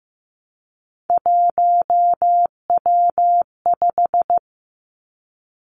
Morse code, heard as a keyed tone, about 15 wpm, 700 Hz. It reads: 1W5
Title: Three Challenging Letter-Number Combinations 15wpm